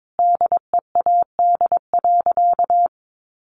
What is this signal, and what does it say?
Morse code, heard as a keyed tone, about 22 wpm, 700 Hz. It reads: DEAD.
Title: A Sound Of Thunder 22wpm